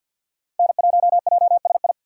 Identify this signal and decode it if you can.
Morse code, heard as a keyed tone, about 62 wpm, 700 Hz. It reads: N1JSI
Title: Speed Racing - USA Call Signs 50wpm